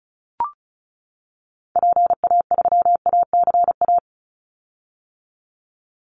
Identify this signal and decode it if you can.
Morse code, heard as a keyed tone, about 34 wpm, 700 Hz. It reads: PA3ACA
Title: Call Signs - 35wpm